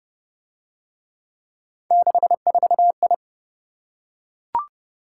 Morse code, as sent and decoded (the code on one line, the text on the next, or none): -.... ....- ..
64I